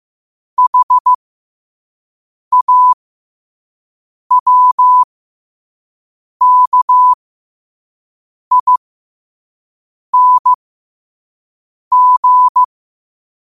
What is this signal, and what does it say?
Morse code, heard as a keyed tone, about 15 wpm, spaced as for 5 wpm, 1000 Hz. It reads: HAWKING
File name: qotdfiles / 5wpm